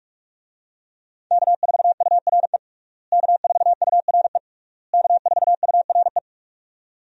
Morse code, heard as a keyed tone, about 42 wpm, 700 Hz. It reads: K4URE K4URE K4URE
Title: USA Call Signs - 5 Characters - Repeated 3x - 45wpm